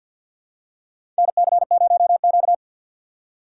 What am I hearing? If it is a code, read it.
Morse code, heard as a keyed tone, about 47 wpm, 700 Hz. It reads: NC0Q
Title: USA Call Signs - 4 Characters - 2x Character Spacing - 50wpm